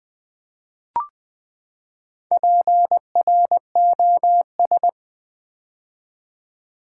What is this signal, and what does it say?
Morse code, heard as a keyed tone, about 20 wpm, 700 Hz. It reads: PROS
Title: Build A Word - 4 Letters At A Time - 11-Letter Words - 20wpm